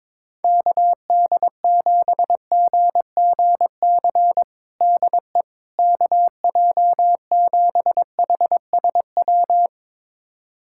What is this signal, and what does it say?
Morse code, heard as a keyed tone, about 22 wpm, 700 Hz. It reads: KD7GGC DE KJ7HSW
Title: USA Callsign DE Callsign - 6 Characters - 22wpm